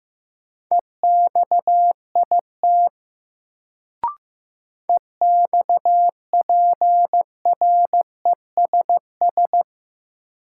Morse code, heard as a keyed tone, about 15 wpm, 700 Hz. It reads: EXIT EXPRESS
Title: Sentences from Top 1100 Words - New Word Drill - 15wpm